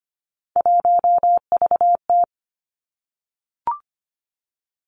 Morse code, heard as a keyed tone, about 25 wpm, 700 Hz. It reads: 1VT
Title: Three Challenging Letter-Number Combinations 25wpm